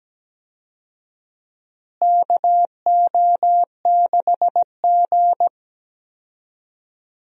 Morse code, heard as a keyed tone, about 17 wpm, 700 Hz. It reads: KO6G
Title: USA Call Signs - 4 Characters - Repeated 3x - 17wpm